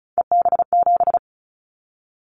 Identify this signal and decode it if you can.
Morse code, heard as a keyed tone, about 36 wpm, 700 Hz. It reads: EB7
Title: Callsign Prefixes for Top 25 DXCC - 35wpm